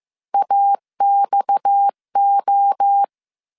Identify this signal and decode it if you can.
Morse code, heard as a keyed tone, about 15 wpm, 800 Hz. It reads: AXO